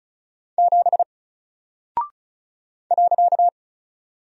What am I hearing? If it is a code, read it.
Morse code, heard as a keyed tone, about 36 wpm, 700 Hz. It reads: 7 .
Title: Single Character (Letter:Number:Punctuation:Prosign) Alt 35wpm